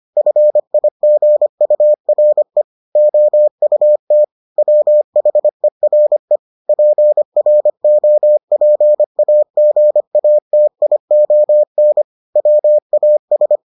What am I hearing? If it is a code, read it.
Morse code, heard as a keyed tone, about 25 wpm, 600 Hz. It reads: FIGURE OUT WHERE PROPAGATION WAS